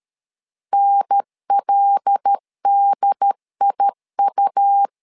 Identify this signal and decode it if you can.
Morse code, heard as a keyed tone, about 13 wpm, 800 Hz. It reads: NLDIU
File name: LEZIONI / 5 LBVC / 5 L 12wpm 1913